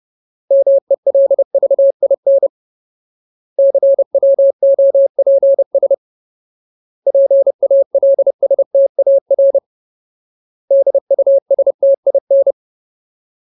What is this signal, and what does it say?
Morse code, heard as a keyed tone, about 30 wpm, 550 Hz. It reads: MELVIN CWOPS PALSTAR DUSTIN
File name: Ultimate Word List / 30WPM Ultimate